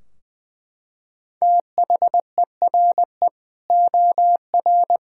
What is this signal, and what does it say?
Morse code, heard as a keyed tone, about 20 wpm, 700 Hz. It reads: THERE OR
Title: Sentences from Top 1600 Words 20wpm